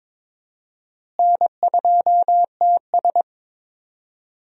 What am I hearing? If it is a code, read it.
Morse code, heard as a keyed tone, about 22 wpm, 700 Hz. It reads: N2TS
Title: USA Call Signs - 4 Characters - Repeated 3x - 22wpm